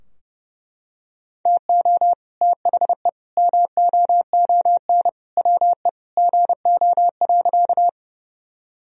Morse code, heard as a keyed tone, about 32 wpm, 700 Hz. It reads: TO THE MOON WE GO.
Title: Sentences from Top 1000 Words - Review Part 3 - 30wpm